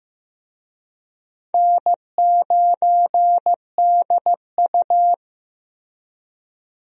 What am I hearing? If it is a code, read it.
Morse code, heard as a keyed tone, about 15 wpm, 700 Hz. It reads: N9DU
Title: USA Call Signs - 4 Characters - 15wpm